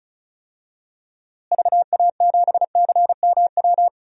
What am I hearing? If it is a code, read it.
Morse code, heard as a keyed tone, about 36 wpm, 700 Hz. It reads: VA7CMW
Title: DXCC-100 Call Signs - 6 Characters - Repeated 3x - 35wpm